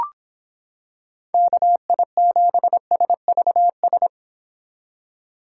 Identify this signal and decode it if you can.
Morse code, heard as a keyed tone, about 26 wpm, 700 Hz. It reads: KI7SVS